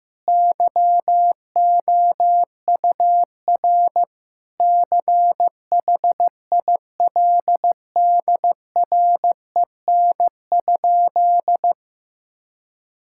Morse code, heard as a keyed tone, about 15 wpm, 700 Hz. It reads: YOUR CHILDREN?
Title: Sentences from Top 1000 Words - Review Part 6 - 15wpm